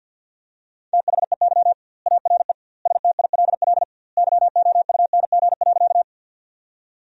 Morse code, heard as a keyed tone, about 50 wpm, 700 Hz. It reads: THEY ARE STILL YOUNG.
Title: 2x Word-Spacing - Sentences from Top 400 Words - 50wpm